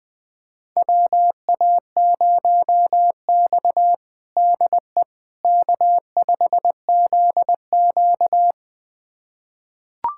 Morse code, heard as a keyed tone, about 20 wpm, 700 Hz. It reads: WA0X DE K5ZQ
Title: USA Callsign DE Callsign - 4 Characters - 20wpm